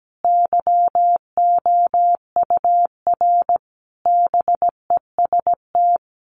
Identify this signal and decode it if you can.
Morse code, heard as a keyed tone, about 17 wpm, 700 Hz. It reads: YOUR BEST
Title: Sentences from Top 1200 Words 17wpm